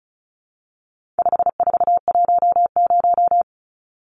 Morse code, heard as a keyed tone, about 34 wpm, 700 Hz. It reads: 5410
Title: Numbers - 4-Digit 35wpm